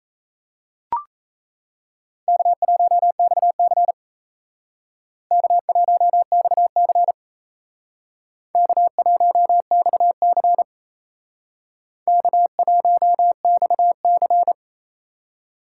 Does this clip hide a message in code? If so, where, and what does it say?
Yes: Morse code from 2.3 to 14.5 s, heard as a keyed tone, about 36 wpm, 700 Hz. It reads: K1XC K1XC K1XC K1XC